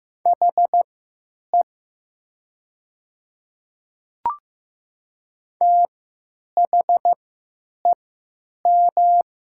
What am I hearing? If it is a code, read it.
Morse code, heard as a keyed tone, about 15 wpm, spaced as for 9 wpm, 690 Hz. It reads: HE THEM
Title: 3x Character Spacing - Top 100 Words - 15wpm